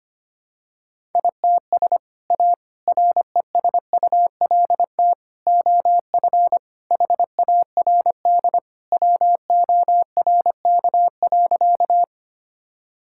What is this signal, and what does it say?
Morse code, heard as a keyed tone, about 25 wpm, 700 Hz. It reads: ITS A RESULT OF HARD WORK.
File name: practice / 25wpm / Sentences with 3-Word Phrases - 25wpm